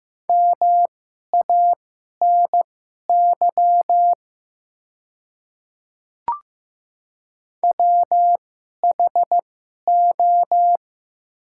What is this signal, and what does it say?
Morse code, heard as a keyed tone, about 16 wpm, spaced as for 11 wpm, 700 Hz. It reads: MANY WHO
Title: 2x Character Spacing - Top 300 Words - 15wpm